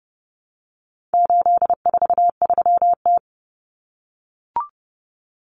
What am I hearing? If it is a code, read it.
Morse code, heard as a keyed tone, about 30 wpm, 700 Hz. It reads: 843T
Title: Common Cut Numbers - 4-Digit 30wpm